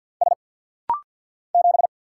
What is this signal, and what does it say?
Morse code, heard as a keyed tone, about 50 wpm, 700 Hz. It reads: S 7
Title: Single Letter-Number Alt - Rapid-Fire 50wpm